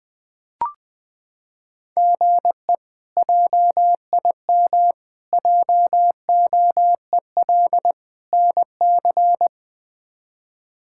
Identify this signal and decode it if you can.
Morse code, heard as a keyed tone, about 20 wpm, 700 Hz. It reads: GE JIM JOEL NC
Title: K1USN SST QSO Practice - 20wpm